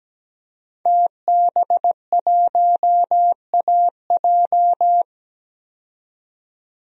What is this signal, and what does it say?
Morse code, heard as a keyed tone, about 17 wpm, 700 Hz. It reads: TB1AJ
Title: DXCC-100 Call Signs - 5 Characters - 17wpm